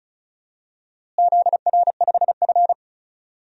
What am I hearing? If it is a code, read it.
Morse code, heard as a keyed tone, about 34 wpm, 700 Hz. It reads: ZR5F
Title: DXCC-100 Call Signs - 4 Characters - Repeated 2x - 35wpm